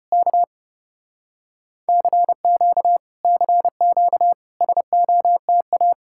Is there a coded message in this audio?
Yes: Morse code from 0.1 to 5.9 s, heard as a keyed tone, about 35 wpm, 700 Hz. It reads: K CQ CQ SOTA